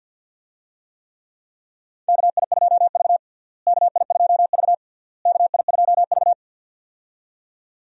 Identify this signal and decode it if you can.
Morse code, heard as a keyed tone, about 52 wpm, 700 Hz. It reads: KI2V KI2V KI2V
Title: USA Call Signs - 4 Characters - Repeated 3x - 50wpm